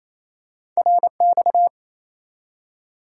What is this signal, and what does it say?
Morse code, heard as a keyed tone, about 28 wpm, 700 Hz. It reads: RX
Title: Top 90 QSO Elements 28wpm